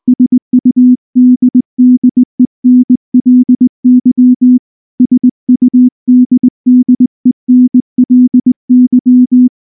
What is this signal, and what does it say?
Morse code, heard as a keyed tone, about 19 wpm, 250 Hz. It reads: SUDDENLY SUDDENLY
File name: morse / morse-training-20WPM-9